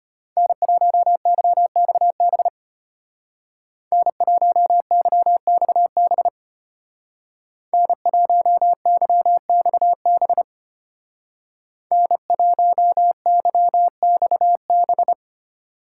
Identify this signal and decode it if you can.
Morse code, heard as a keyed tone, about 32 wpm, 700 Hz. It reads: N1YXB N1YXB N1YXB N1YXB